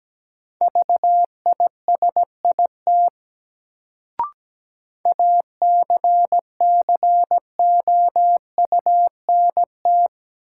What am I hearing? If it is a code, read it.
Morse code, heard as a keyed tone, about 17 wpm, 700 Hz. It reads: VISIT ACCOUNT
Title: Sentences from Top 900 Words - New Word Drill - 17wpm